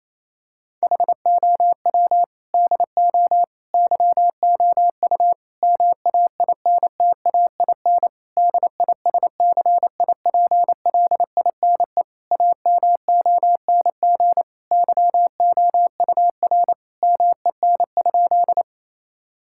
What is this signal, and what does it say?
Morse code, heard as a keyed tone, about 28 wpm, 700 Hz. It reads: HOW DO YOU MAINTAIN DISCIPLINE AMONG YOUR MEN?
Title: Sentences from Top 1000 Words - Review Part 6 - 28wpm